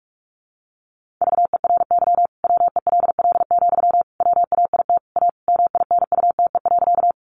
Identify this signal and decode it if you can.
Morse code, heard as a keyed tone, about 44 wpm, 700 Hz. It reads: VERY WELL, WAIT A MINUTE.
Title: Sentences from Top 500 Words 45wpm